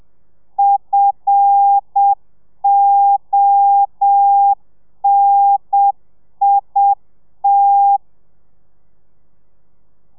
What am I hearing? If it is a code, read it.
Morse code, heard as a keyed tone, about 7 wpm, 800 Hz. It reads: FONIT
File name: telegrafia / i2tfj / Lez5 30